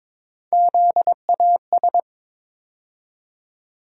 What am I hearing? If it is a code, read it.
Morse code, heard as a keyed tone, about 22 wpm, 700 Hz. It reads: ZAS
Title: USA Call Sign Postfixes - 3 Letters 22wpm